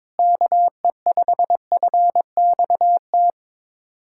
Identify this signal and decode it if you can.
Morse code, heard as a keyed tone, about 22 wpm, 700 Hz. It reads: KE5FXT
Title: USA Callsign DE Callsign - 6 Characters - 22wpm